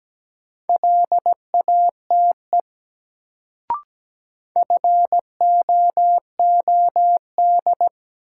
Morse code, heard as a keyed tone, about 17 wpm, 700 Hz. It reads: LATE FOOD
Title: Sentences from Top 300 Words - New Word Drill - 17wpm